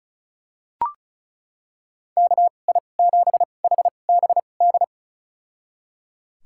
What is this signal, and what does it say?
Morse code, heard as a keyed tone, about 36 wpm, spaced as for 25 wpm, 700 Hz. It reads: KI7HBD
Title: USA Call Signs - 6 Characters - 2x Character Spacing - 35wpm